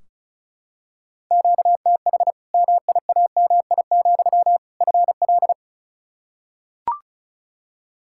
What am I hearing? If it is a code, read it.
Morse code, heard as a keyed tone, about 36 wpm, 700 Hz. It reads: QTH MIAMI, FL
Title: Practice QSO - Exchange 1 - Beginning - 35wpm